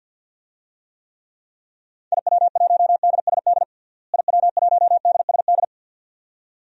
Morse code, heard as a keyed tone, about 53 wpm, 700 Hz. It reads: IW1DSD IW1DSD
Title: DXCC-100 Call Signs - 6 Characters - Repeated 2x - 50wpm